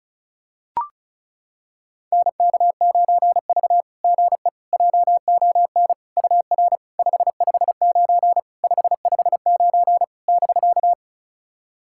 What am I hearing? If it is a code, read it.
Morse code, heard as a keyed tone, about 36 wpm, 690 Hz. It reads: NK9V GE JON UR 559 559 <BK>
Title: SOTA QSO Practice 35wpm